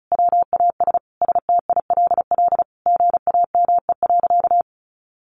Morse code, heard as a keyed tone, about 34 wpm, 700 Hz. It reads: WAS STILL GAME.
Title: Sentences from Top 500 Words 35wpm